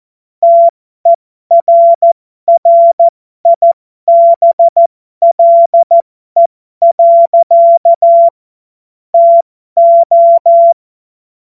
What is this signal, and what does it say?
Morse code, heard as a keyed tone, about 14 wpm, spaced as for 12 wpm, 680 Hz. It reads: TERRIBLE. TO